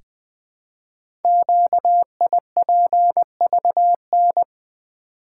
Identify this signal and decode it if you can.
Morse code, heard as a keyed tone, about 20 wpm, 700 Hz. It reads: QIPVN
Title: Code Groups - 5 Letter:Numbers - 20wpm